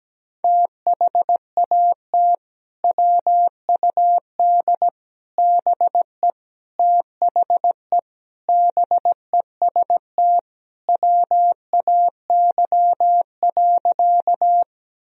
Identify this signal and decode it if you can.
Morse code, heard as a keyed tone, about 17 wpm, 700 Hz. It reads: THAT WUD BE THE BEST WAY.
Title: Sentences from Top 400 Words and Abbreviations 17wpm